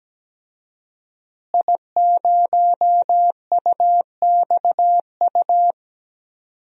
Morse code, heard as a keyed tone, about 17 wpm, 700 Hz. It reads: I0UXU
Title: DXCC-100 Call Signs - 5 Characters - Repeated 2x - 17wpm